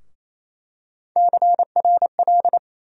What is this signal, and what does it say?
Morse code, heard as a keyed tone, about 28 wpm, 700 Hz. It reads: CRL
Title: USA Call Sign Postfixes - 3 Letters 28wpm